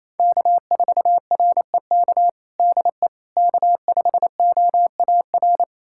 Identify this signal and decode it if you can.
Morse code, heard as a keyed tone, about 28 wpm, 700 Hz. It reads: K4REK DE K5OAR
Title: USA Callsign DE Callsign - 5 Characters - 28wpm